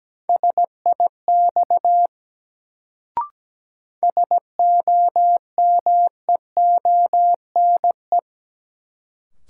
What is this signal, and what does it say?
Morse code, heard as a keyed tone, about 17 wpm, 700 Hz. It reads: SIX SOMEONE
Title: Sentences from Top 600 Words - New Word Drill - 17wpm